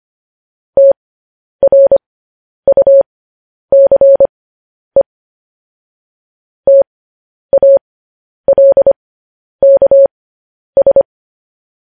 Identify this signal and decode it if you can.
Morse code, heard as a keyed tone, about 25 wpm, spaced as for 10 wpm, 550 Hz. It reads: TRUCE TALKS